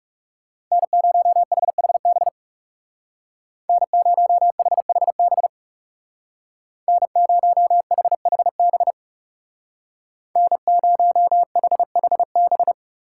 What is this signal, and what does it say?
Morse code, heard as a keyed tone, about 36 wpm, 700 Hz. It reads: N0HHB N0HHB N0HHB N0HHB
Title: Speed Racing - USA 1x3 Call Signs 30wpm